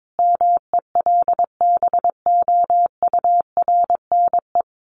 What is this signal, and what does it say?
Morse code, heard as a keyed tone, about 22 wpm, 700 Hz. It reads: MELBOURNE
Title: Common QSO Phrases 22wpm